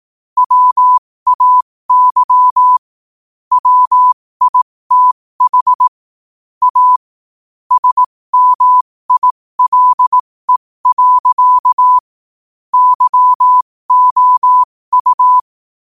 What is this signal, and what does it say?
Morse code, heard as a keyed tone, about 18 wpm, spaced as for 15 wpm, 1000 Hz. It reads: WAY WITH A SMILE. YOU
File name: qotdfiles / 15wpm